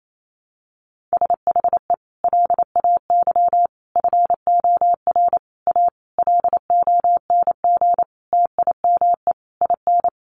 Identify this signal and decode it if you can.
Morse code, heard as a keyed tone, about 28 wpm, 700 Hz. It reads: SHE LAY FOR A LONG TIME IN
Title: Sentences from Top 900 Words 28wpm